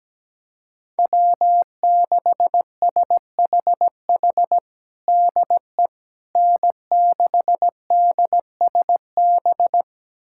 Morse code, heard as a keyed tone, about 17 wpm, 700 Hz. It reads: W6SHH DE N6DSB
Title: USA Callsign DE Callsign - 5 Characters - 17wpm